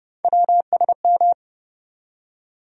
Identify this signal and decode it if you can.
Morse code, heard as a keyed tone, about 30 wpm, 700 Hz. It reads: WSM
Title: USA Call Sign Postfixes - 3 Letters 30wpm